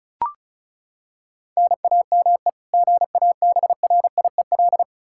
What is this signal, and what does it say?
Morse code, heard as a keyed tone, about 36 wpm, 700 Hz. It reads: NAME GABRIEL
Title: K1USN SST QSO Practice 35wpm